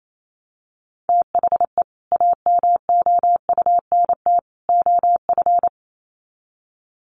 Morse code, heard as a keyed tone, about 28 wpm, 700 Hz. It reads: THE AMOUNT OF
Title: Sets of 3 Words 28wpm